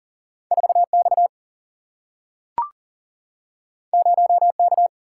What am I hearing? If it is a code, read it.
Morse code, heard as a keyed tone, about 40 wpm, 700 Hz. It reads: 4 X 0 K
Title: Single-Letter Single-Number 40wpm